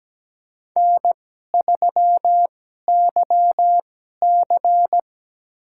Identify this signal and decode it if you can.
Morse code, heard as a keyed tone, about 17 wpm, spaced as for 12 wpm, 700 Hz. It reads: N3YC
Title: USA Call Signs - 4 Characters - 2x Character Spacing - 17wpm